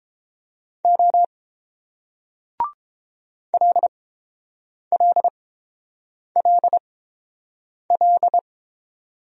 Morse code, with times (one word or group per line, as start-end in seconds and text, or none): 0.8-1.2 O
3.5-8.4 LLLL